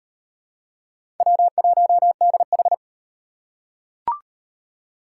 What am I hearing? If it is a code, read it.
Morse code, heard as a keyed tone, about 34 wpm, 700 Hz. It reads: W1DH